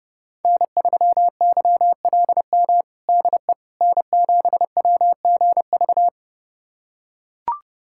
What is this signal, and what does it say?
Morse code, heard as a keyed tone, about 30 wpm, 700 Hz. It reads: N3YLM DE N7WGV
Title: USA Callsign DE Callsign - 5 Characters - 30wpm